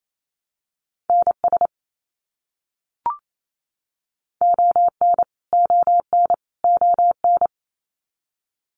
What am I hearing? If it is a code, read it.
Morse code, heard as a keyed tone, about 28 wpm, 700 Hz. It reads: NS ON ON ON
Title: Canadian Provinces and Territories - Repeated 3x - 28wpm